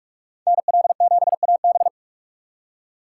Morse code, heard as a keyed tone, about 46 wpm, 700 Hz. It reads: NL7AB
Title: USA Call Signs - 5 Characters - 2x Character Spacing - 45wpm